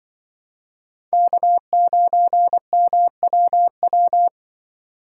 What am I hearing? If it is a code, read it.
Morse code, heard as a keyed tone, about 24 wpm, 700 Hz. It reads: K9MWW
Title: Speed Racing - USA 1x3 Call Signs 20wpm